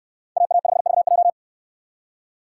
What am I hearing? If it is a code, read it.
Morse code, heard as a keyed tone, about 64 wpm, 700 Hz. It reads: AA5VJ